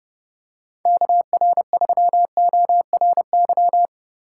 Morse code, heard as a keyed tone, about 30 wpm, 700 Hz. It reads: KR3ORY
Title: USA Call Signs - 6 Characters - 6x Character Spacing - 30wpm